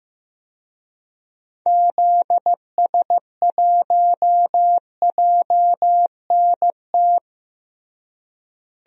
Morse code, heard as a keyed tone, about 16 wpm, 700 Hz. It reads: ZS1JNT
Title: DXCC-100 Call Signs - 6 Characters - Repeated 3x - 15wpm